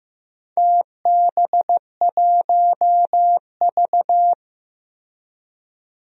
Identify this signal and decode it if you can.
Morse code, heard as a keyed tone, about 15 wpm, 700 Hz. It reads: TB1V
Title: DXCC-100 Call Signs - 4 Characters - Repeated 2x - 15wpm